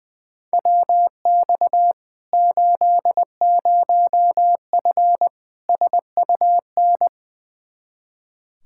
Morse code, heard as a keyed tone, about 20 wpm, 700 Hz. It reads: WX 80F SUN
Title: Practice QSO - Exchange 2 - Info - 20wpm